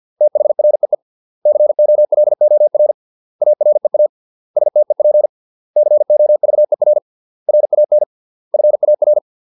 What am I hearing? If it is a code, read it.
Morse code, heard as a keyed tone, about 50 wpm, 600 Hz. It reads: THREE COLOR AREA STEP COVER RAN FAR
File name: 632 words / 632 Words 50WPM